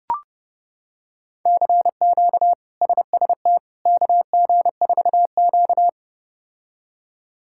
Morse code, heard as a keyed tone, about 30 wpm, 700 Hz. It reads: CQ SST KG4Q
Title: K1USN SST QSO Practice - 30wpm